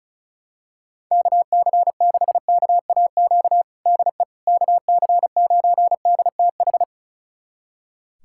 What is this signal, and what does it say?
Morse code, heard as a keyed tone, about 36 wpm, 700 Hz. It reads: KC6KAQ DE KC9DTH
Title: USA Callsign DE Callsign - 6 Characters - 35wpm